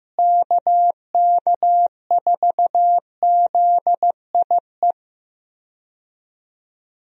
Morse code, heard as a keyed tone, about 15 wpm, 700 Hz. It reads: KK4ZIE
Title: USA Call Signs - 6 Characters - Repeated 3x - 15wpm